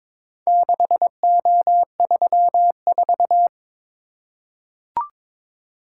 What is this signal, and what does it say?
Morse code, heard as a keyed tone, about 22 wpm, 700 Hz. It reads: 6O34
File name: practice / 22wpm / Four Letter-Number Combinations - 22wpm